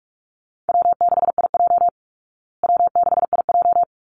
Morse code, heard as a keyed tone, about 44 wpm, 700 Hz. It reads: W6IJ W6IJ
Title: Call Signs Repeated 2x - 45wpm